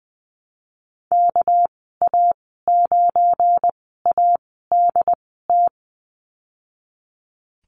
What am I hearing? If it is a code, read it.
Morse code, heard as a keyed tone, about 20 wpm, spaced as for 15 wpm, 700 Hz. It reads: KA9ADT
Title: 245 Lesson 39g - Letter z - Call Signs 20wpm